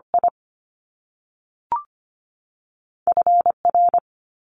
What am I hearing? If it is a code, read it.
Morse code, heard as a keyed tone, about 25 wpm, spaced as for 3 wpm, 700 Hz. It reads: IF R
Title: Top Two Letter Combinations in Words 25wpm